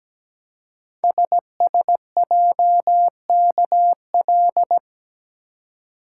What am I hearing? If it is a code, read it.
Morse code, heard as a keyed tone, about 17 wpm, 700 Hz. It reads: SSJKL
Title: Code Groups - 5 Letters - 17wpm